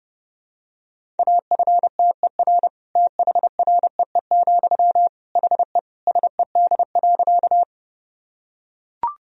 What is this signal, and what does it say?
Morse code, heard as a keyed tone, about 32 wpm, 700 Hz. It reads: AFTER THREE, HE SED.
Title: Sentences from Top 200 Words and Abbreviations 30wpm